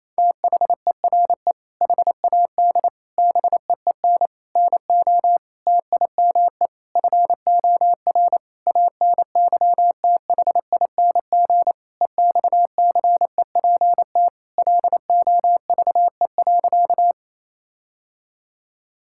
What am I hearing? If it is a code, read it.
Morse code, heard as a keyed tone, about 28 wpm, 700 Hz. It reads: THERE HAD BEEN NO TIME FOR ANYTHING EXCEPT LOVE.
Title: Sentences from Top 1600 Words 28wpm